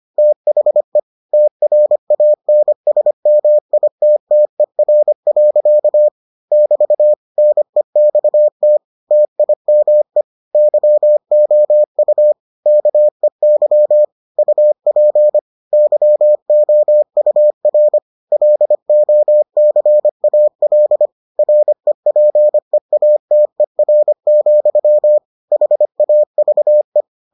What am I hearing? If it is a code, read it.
Morse code, heard as a keyed tone, about 25 wpm, 600 Hz. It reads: THE TRANSMITTER. <BT> NEXT TIME YOU KEY UP YOUR LOCAL REPEATER, HAVE